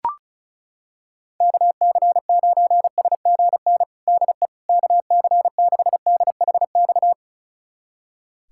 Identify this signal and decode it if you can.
Morse code, heard as a keyed tone, about 36 wpm, 700 Hz. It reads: KC9SGN DE KC6DHX
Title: USA Callsign DE Callsign 35wpm